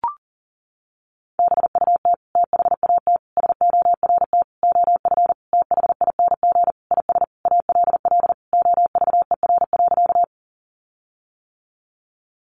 Morse code, heard as a keyed tone, about 40 wpm, 700 Hz. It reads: BUT THAT SORT OF THING IS ALL OVER.
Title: Sentences from Top 1300 Words 40wpm